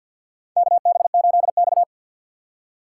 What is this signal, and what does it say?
Morse code, heard as a keyed tone, about 53 wpm, 700 Hz. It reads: KB8X
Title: USA Call Signs - 4x Character Spacing - 50wpm